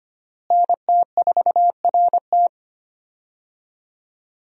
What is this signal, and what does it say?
Morse code, heard as a keyed tone, about 25 wpm, 700 Hz. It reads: NT4RT
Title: USA Call Signs - 5 Characters - Repeated 3x - 25wpm